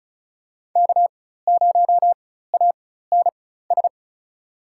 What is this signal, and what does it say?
Morse code, heard as a keyed tone, about 36 wpm, spaced as for 16 wpm, 700 Hz. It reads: K0ANS
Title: USA Call Signs - 5 Characters - 4x Character Spacing - 35wpm